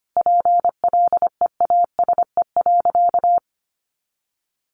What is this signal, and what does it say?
Morse code, heard as a keyed tone, about 25 wpm, 700 Hz. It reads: PLEASE.
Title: Sentences from Top 1500 Words 25wpm